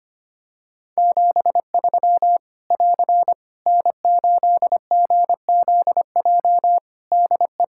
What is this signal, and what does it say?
Morse code, heard as a keyed tone, about 25 wpm, 700 Hz. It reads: 73 <AR> N8GZJ DE
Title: Practice QSO - Exchange 4 - Ending - 25wpm